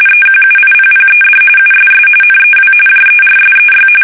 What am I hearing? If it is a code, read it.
RTTY, about 45.45 baud. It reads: EQUIPPED USS NEW JERSEY
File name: fsk850 / battleshipnj